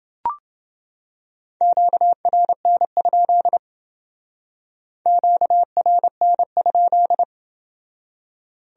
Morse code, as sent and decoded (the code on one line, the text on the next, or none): --.- .-. -. ..--.. / --.- .-. -. ..--..
QRN? QRN?